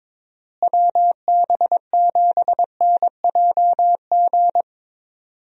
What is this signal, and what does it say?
Morse code, heard as a keyed tone, about 22 wpm, 700 Hz. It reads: WB7NJG
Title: USA Call Signs - 6 Characters - Repeated 3x - 22wpm